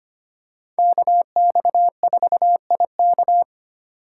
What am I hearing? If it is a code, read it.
Morse code, heard as a keyed tone, about 25 wpm, 700 Hz. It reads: KX4IK
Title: USA Call Signs - 5 Characters - 6x Character Spacing - 25wpm